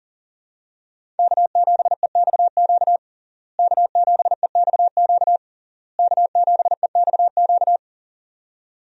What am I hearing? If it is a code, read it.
Morse code, heard as a keyed tone, about 40 wpm, 700 Hz. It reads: K7EXQ K7EXQ K7EXQ
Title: USA Call Signs - 5 Characters - Repeated 3x - 40wpm